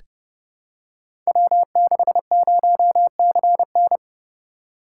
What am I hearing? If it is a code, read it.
Morse code, heard as a keyed tone, about 29 wpm, 700 Hz. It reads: W60CN
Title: Code Groups - 5 Letter:Numbers - 30wpm